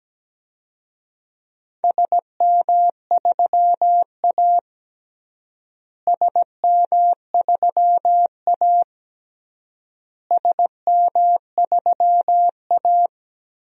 Morse code, heard as a keyed tone, about 17 wpm, 700 Hz. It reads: SM3A SM3A SM3A
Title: DXCC-100 Call Signs - 4 Characters - Repeated 3x - 17wpm